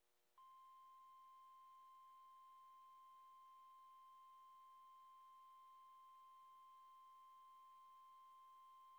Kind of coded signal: Emergency Alert System (SAME) tones